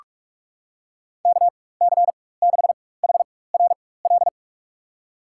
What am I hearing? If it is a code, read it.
Morse code, heard as a keyed tone, about 44 wpm, spaced as for 21 wpm, 700 Hz. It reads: KC6HRL
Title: USA Call Signs - 4x Character Spacing - 45wpm